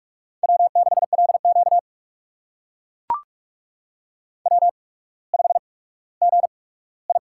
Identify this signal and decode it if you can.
Morse code, heard as a keyed tone, about 44 wpm, spaced as for 19 wpm, 690 Hz. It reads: W6LQ W5GI
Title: USA Call Signs - 4 Characters - 8x Character Spacing - 45wpm